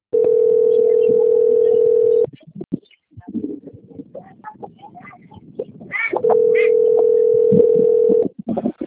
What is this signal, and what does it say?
Ringback tone (North America), 440 + 480 Hz.